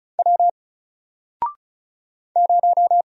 Morse code, heard as a keyed tone, about 34 wpm, 700 Hz. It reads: W 0
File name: Single-Letter-Number-35wpm